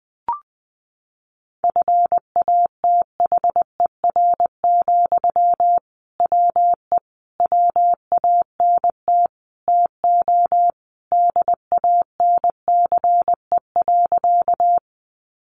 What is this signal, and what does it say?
Morse code, heard as a keyed tone, about 20 wpm, 700 Hz. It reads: FATHER, WE WANT TO DANCE.
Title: Sentences from Top 700 Words 20wpm